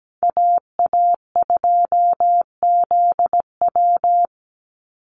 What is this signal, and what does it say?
Morse code, heard as a keyed tone, about 17 wpm, 700 Hz. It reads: AA2ZW
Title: Call Signs Repeated 2x - 17wpm